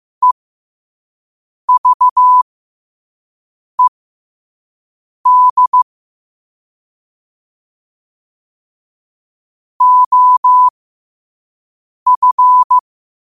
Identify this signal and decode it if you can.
Morse code, heard as a keyed tone, about 15 wpm, spaced as for 5 wpm, 1000 Hz. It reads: EVED OF
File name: qotdfiles / 5wpm